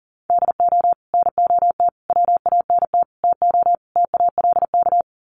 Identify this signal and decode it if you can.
Morse code, heard as a keyed tone, about 37 wpm, 700 Hz. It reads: DO NOT WANT TO TALK
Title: Sets of 5 Words 40wpm